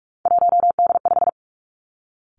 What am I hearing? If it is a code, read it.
Morse code, heard as a keyed tone, about 46 wpm, 700 Hz. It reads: 1D5
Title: Three Challenging Letter-Number Combinations 45wpm